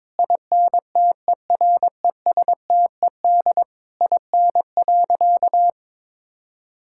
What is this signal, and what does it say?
Morse code, heard as a keyed tone, about 22 wpm, 700 Hz. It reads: INTERESTED IN.
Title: Sentences from Top 1000 Words - Review Part 6 - 22wpm